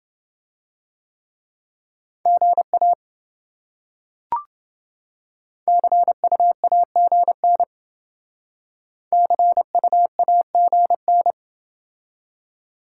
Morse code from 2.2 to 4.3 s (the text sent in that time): GA